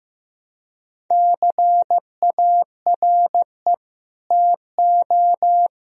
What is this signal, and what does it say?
Morse code, heard as a keyed tone, about 15 wpm, 700 Hz. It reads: CARE TO
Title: Sentences from Top 1000 Words - Review Part 1 - 15wpm